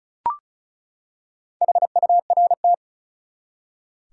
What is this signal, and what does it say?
Morse code, heard as a keyed tone, about 34 wpm, 700 Hz. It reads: HURT
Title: Four Letter Words - 35wpm